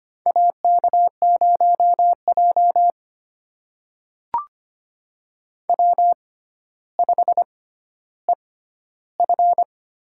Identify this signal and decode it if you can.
Morse code, heard as a keyed tone, about 25 wpm, spaced as for 13 wpm, 700 Hz. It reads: AK0J W5EF
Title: USA Call Signs - 4 Characters - 6x Character Spacing - 25wpm